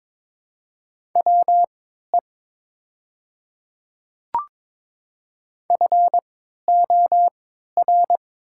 Morse code, heard as a keyed tone, about 22 wpm, spaced as for 13 wpm, 700 Hz. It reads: WE FOR
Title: 3x Character Spacing - Top 200 Words - 22wpm